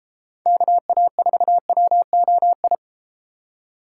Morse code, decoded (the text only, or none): KA4WOI